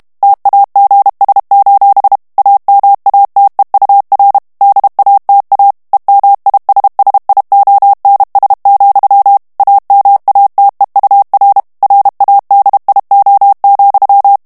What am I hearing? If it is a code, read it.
Morse code, heard as a keyed tone, about 32 wpm, 800 Hz. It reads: TAGS: AMATEUR DATA EMISSIONS, AMATEUR RADIO,
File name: ANGLAIS texte clair 3 vitesse 30 mots minute